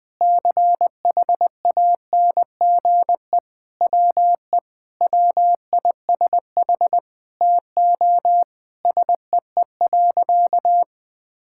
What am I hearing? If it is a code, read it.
Morse code, heard as a keyed tone, about 20 wpm, 700 Hz. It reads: CHANGE WE WISH TO SEE.
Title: Sentences with 8-Word Phrases - 20wpm